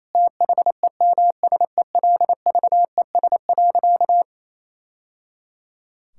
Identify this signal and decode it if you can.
Morse code, heard as a keyed tone, about 28 wpm, 700 Hz. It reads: THEMSELVES.
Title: Sentences from Top 1000 Words - Review Part 5 - 28wpm